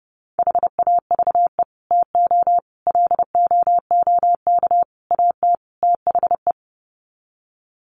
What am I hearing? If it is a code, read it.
Morse code, heard as a keyed tone, about 30 wpm, 700 Hz. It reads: HAVE TO LOOK AT THE
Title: Sets of 5 Words Encore 30wpm